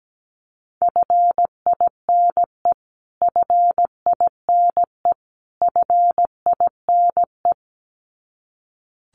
Morse code, heard as a keyed tone, about 17 wpm, 700 Hz. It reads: FINE FINE FINE
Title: Top 500 Words Repeated 17wpm